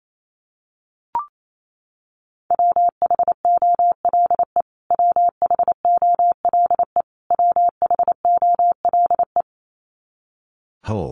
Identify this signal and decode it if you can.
Morse code, heard as a keyed tone, about 28 wpm, 700 Hz. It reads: WHOLE WHOLE WHOLE
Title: Top 500 Words Repeated 28wpm